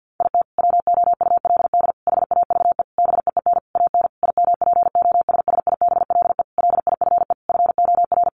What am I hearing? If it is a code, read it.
Morse code, heard as a keyed tone, about 50 wpm, 700 Hz. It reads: IT WOULD HAVE BEEN AN IMPOSSIBLE LIFE FOR
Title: Sentences from Top 1100 Words 50wpm